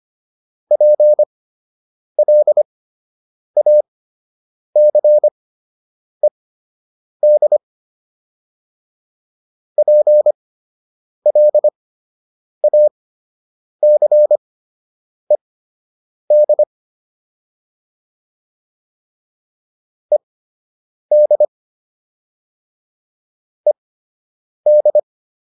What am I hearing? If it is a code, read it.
Morse code, heard as a keyed tone, about 25 wpm, spaced as for 8 wpm, 600 Hz. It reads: PLACED PLACED ED ED
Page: fundamental-practice-files